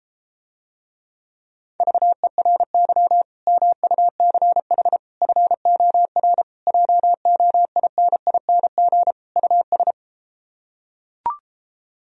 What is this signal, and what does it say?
Morse code, heard as a keyed tone, about 33 wpm, 700 Hz. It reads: VERY MUCH FOR JOINING US